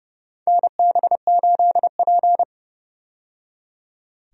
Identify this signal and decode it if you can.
Morse code, heard as a keyed tone, about 30 wpm, 700 Hz. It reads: NB8P
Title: USA Call Signs - 4 Characters - 30wpm